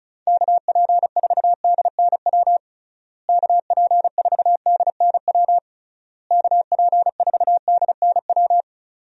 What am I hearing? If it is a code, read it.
Morse code, heard as a keyed tone, about 36 wpm, 700 Hz. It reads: KP4DNW KP4DNW KP4DNW
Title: USA Call Signs - 6 Characters - Repeated 3x - 35wpm